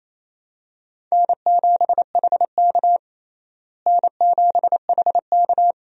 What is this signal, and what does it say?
Morse code, heard as a keyed tone, about 28 wpm, 700 Hz. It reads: N7HK N7HK
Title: USA Call Signs - 4 Characters - Repeated 2x - 28wpm